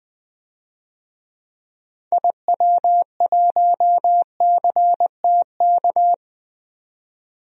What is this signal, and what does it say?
Morse code, heard as a keyed tone, about 20 wpm, 700 Hz. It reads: IW1CTK
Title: DXCC-100 Call Signs - 6 Characters - 20wpm